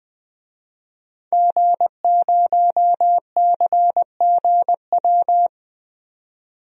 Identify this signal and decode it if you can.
Morse code, heard as a keyed tone, about 20 wpm, 700 Hz. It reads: G0CGW
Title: DXCC-100 Call Signs - 5 Characters - 20wpm